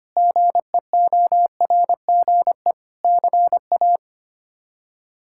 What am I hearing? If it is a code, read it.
Morse code, heard as a keyed tone, about 25 wpm, 700 Hz. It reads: GEORGE CA
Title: K1USN SST QSO Practice - 25wpm